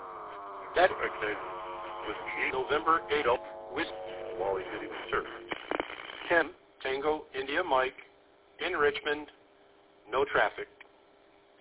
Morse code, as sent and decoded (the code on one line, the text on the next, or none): none